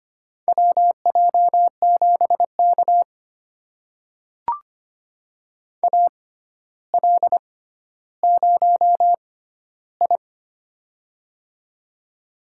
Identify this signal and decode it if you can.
Morse code, heard as a keyed tone, about 25 wpm, spaced as for 13 wpm, 700 Hz. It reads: WJ7K AL0I
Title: USA Call Signs - 4 Characters - 6x Character Spacing - 25wpm